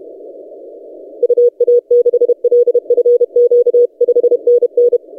Morse code, heard as a keyed tone, about 30 wpm, 480 Hz. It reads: UA6LFQ 5NN
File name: posts / ri0x-20cw- -0519z